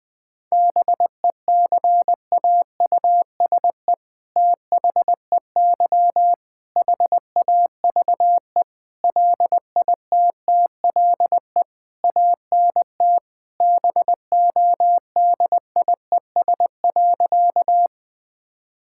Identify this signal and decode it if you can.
Morse code, heard as a keyed tone, about 20 wpm, 700 Hz. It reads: BECAUSE THEY HAVE LITTLE ANT BODIES.
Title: All-Age Appropriate Jokes - 20wpm